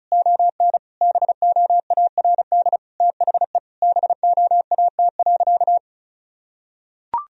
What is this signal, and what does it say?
Morse code, heard as a keyed tone, about 36 wpm, 700 Hz. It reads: ON BOARD THE BOAT.
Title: Sentences from Top 1000 Words - Review Part 4 - 35wpm